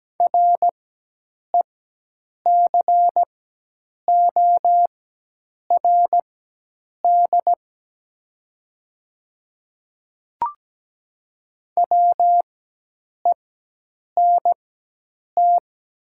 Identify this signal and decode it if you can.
Morse code, heard as a keyed tone, about 17 wpm, spaced as for 8 wpm, 700 Hz. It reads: RECORD WENT
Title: 4x Character Spacing - Top 500 Words - 17wpm